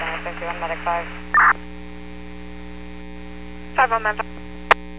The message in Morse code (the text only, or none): none